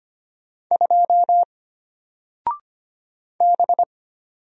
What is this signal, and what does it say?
Morse code, heard as a keyed tone, about 25 wpm, 700 Hz. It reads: 2 B
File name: Single-Letter-Number-25wpm